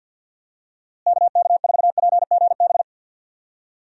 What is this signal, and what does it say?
Morse code, heard as a keyed tone, about 48 wpm, 700 Hz. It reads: KK4PGB
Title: USA Call Signs - 6 Characters - Repeated 2x - 50wpm